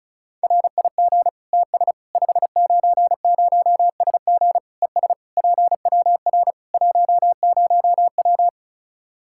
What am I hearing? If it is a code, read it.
Morse code, heard as a keyed tone, about 36 wpm, 700 Hz. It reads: RIG TS 590SG ES PWR 10W
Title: Practice QSO - Exchange 2 - Info - 35wpm